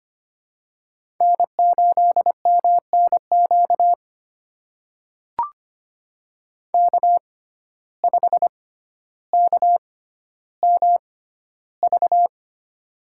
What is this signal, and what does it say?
Morse code, heard as a keyed tone, about 25 wpm, spaced as for 13 wpm, 700 Hz. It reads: N8MNQ K5KMV
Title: USA Call Signs - 5 Characters - 6x Character Spacing - 25wpm